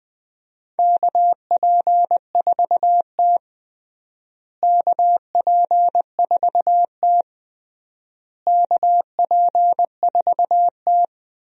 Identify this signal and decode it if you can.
Morse code, heard as a keyed tone, about 21 wpm, 700 Hz. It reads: KP4T KP4T KP4T
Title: DXCC-100 Call Signs - 4 Characters - Repeated 3x - 20wpm